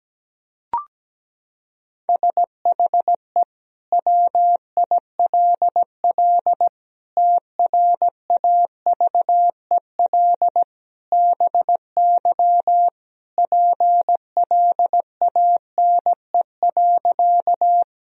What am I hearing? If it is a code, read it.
Morse code, heard as a keyed tone, about 17 wpm, 700 Hz. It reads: SHE WILL TRAVEL BY PLANE.
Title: Sentences from Top 1000 Words - Review Part 3 - 17wpm